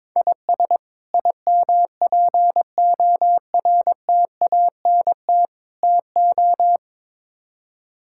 Morse code, heard as a keyed tone, about 22 wpm, 700 Hz. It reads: IS IMPORTANT TO
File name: practice / 22wpm / Build A Phrase - 5-Word Phrases - 22wpm